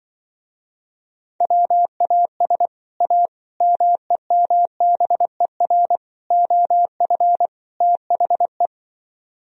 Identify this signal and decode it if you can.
Morse code, heard as a keyed tone, about 24 wpm, 700 Hz. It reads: WAS A MEMBER OF THE
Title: Speed-Racing - Sets of 5 Words 20wpm